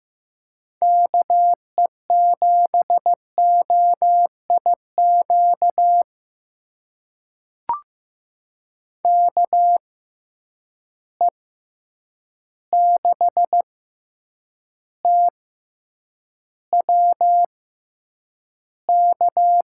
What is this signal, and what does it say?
Morse code, heard as a keyed tone, about 15 wpm, spaced as for 8 wpm, 700 Hz. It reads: KE7OIQ KE6TWK
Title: USA Call Signs - 6 Characters - 6x Character Spacing - 15wpm